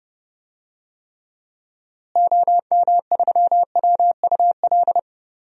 Morse code, heard as a keyed tone, about 30 wpm, 700 Hz. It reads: OM3WUL
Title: DXCC-100 Call Signs - 6 Characters - Repeated 3x - 30wpm